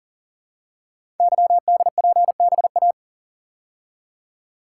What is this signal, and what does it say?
Morse code, heard as a keyed tone, about 40 wpm, 690 Hz. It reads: YDPBA
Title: Code Groups - 5 Letters - 40wpm